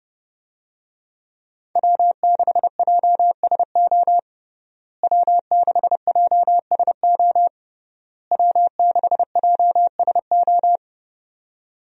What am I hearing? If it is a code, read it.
Morse code, heard as a keyed tone, about 29 wpm, 700 Hz. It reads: W6JSO W6JSO W6JSO
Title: USA Call Signs - 5 Characters - Repeated 3x - 30wpm